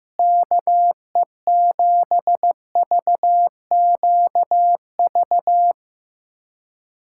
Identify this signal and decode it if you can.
Morse code, heard as a keyed tone, about 15 wpm, 700 Hz. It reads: KE7VQV
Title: USA Callsign DE Callsign - 6 Characters - 15wpm